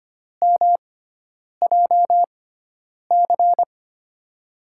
Morse code, heard as a keyed tone, about 25 wpm, spaced as for 9 wpm, 700 Hz. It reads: MJC
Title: USA Call Signs - 5 Characters - 6x Character Spacing - 25wpm